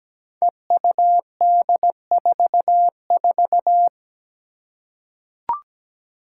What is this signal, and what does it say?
Morse code, heard as a keyed tone, about 17 wpm, 700 Hz. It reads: EUD44
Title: Five Challenging Letter-Number Combinations 17wpm